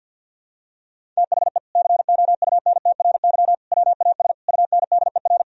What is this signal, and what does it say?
Morse code, heard as a keyed tone, about 51 wpm, 700 Hz. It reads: THE COUNTRY WAS UNDER
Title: Sentences from Top 1000 Words - Review Part 1 - 50wpm